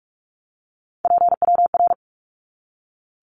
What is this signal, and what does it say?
Morse code, heard as a keyed tone, about 44 wpm, 700 Hz. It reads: PWR
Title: Abbreviations 45wpm